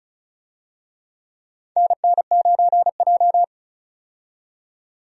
Morse code, heard as a keyed tone, about 36 wpm, 700 Hz. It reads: NN9J
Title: USA Call Signs - 4 Characters - 35wpm